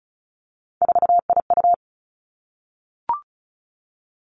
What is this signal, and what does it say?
Morse code, heard as a keyed tone, about 34 wpm, 700 Hz. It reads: 4IU
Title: Three Challenging Letter-Number Combinations 35wpm